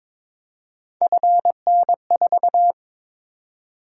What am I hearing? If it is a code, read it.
Morse code, heard as a keyed tone, about 22 wpm, 700 Hz. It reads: FN4
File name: practice / 22wpm / Three Letter-Number Combinations - 22wpm